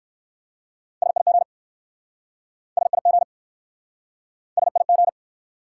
Morse code, heard as a keyed tone, about 58 wpm, 700 Hz. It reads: SIG SIG SIG